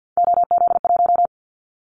Morse code, heard as a keyed tone, about 50 wpm, 700 Hz. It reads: OZ1
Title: Callsign Prefixes for Top 100 DXCC - 50wpm